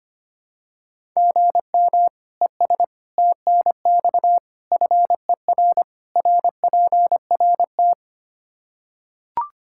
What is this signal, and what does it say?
Morse code, heard as a keyed tone, about 25 wpm, 700 Hz. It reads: GM ES TNX FER RPRT
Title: Practice QSO - Exchange 1 - Beginning - 25wpm